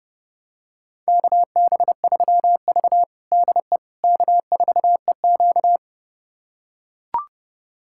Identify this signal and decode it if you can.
Morse code, heard as a keyed tone, about 32 wpm, 700 Hz. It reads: KB3V DE K4EQ
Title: USA Callsign DE Callsign - 4 Characters - 30wpm